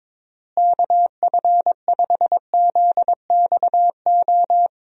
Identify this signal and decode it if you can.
Morse code, heard as a keyed tone, about 22 wpm, 700 Hz. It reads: KF5ZXO
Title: USA Call Signs - 6 Characters - Repeated 3x - 22wpm